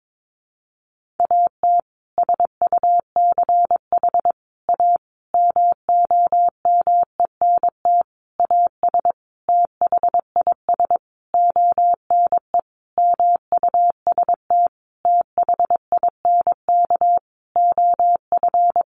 Morse code, heard as a keyed tone, about 22 wpm, 700 Hz. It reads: AT SUCH A MOMENT AS THIS ONE MUST THINK OF
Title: Sentences from Top 1000 Words 22wpm